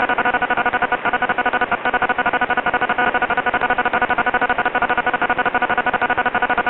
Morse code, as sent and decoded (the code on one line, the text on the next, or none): none